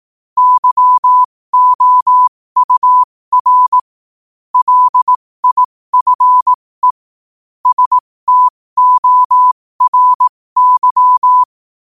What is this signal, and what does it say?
Morse code, heard as a keyed tone, about 18 wpm, spaced as for 15 wpm, 1000 Hz. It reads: YOUR LIFE STORY
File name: qotdfiles / 15wpm